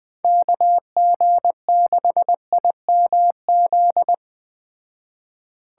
Morse code, heard as a keyed tone, about 20 wpm, 700 Hz. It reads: KG6IMZ